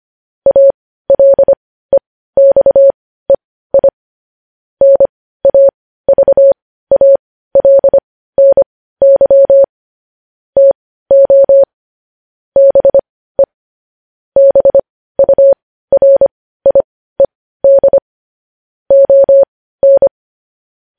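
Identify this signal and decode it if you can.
Morse code, heard as a keyed tone, about 25 wpm, spaced as for 15 wpm, 550 Hz. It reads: ALEXEI NAVALNY TO BE BURIED ON
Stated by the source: News headlines sent as Morse practice at 15.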